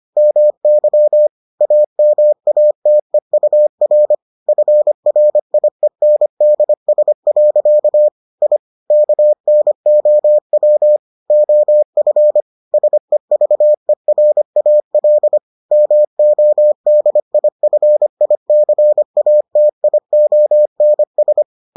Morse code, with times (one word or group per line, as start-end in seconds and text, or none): 0.2-1.3 MY
1.6-4.1 AMATEUR
4.5-8.1 FRIENDS.
8.4-8.6 I
8.9-11.0 KNOW
11.3-12.4 OF
12.7-15.4 SEVERAL
15.7-21.4 MODIFICATIONS